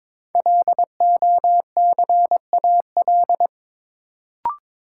Morse code, heard as a keyed tone, about 22 wpm, 700 Hz. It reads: LOCAL
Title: Sentences from Top 1500 Words - New Word Drill Alt - 22wpm